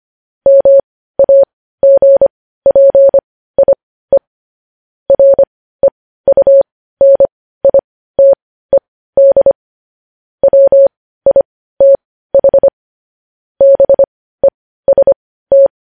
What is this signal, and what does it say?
Morse code, heard as a keyed tone, about 25 wpm, spaced as for 15 wpm, 550 Hz. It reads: MAGPIE REUNITED WITH BEST